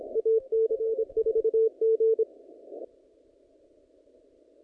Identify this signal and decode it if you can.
Morse code, heard as a keyed tone, about 25 wpm, 450 Hz. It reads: AC4G
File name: contests / naqp-cw-k5tr-op-k5pi- -0000